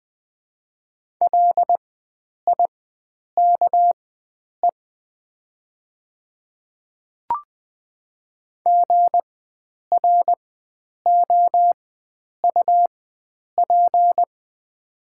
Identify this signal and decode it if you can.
Morse code, heard as a keyed tone, about 20 wpm, spaced as for 9 wpm, 700 Hz. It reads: LIKE GROUP
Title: 4x Character Spacing - Top 300 Words - 20wpm